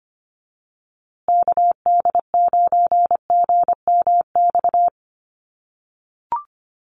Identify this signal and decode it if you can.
Morse code, heard as a keyed tone, about 25 wpm, 700 Hz. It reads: KD9GMX
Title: USA Call Signs Repeated 3x 25wpm